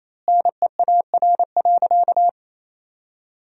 Morse code, heard as a keyed tone, about 28 wpm, 700 Hz. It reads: NEAR.
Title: 4x Word-Spacing - Sentences from Top 200 Words - 28wpm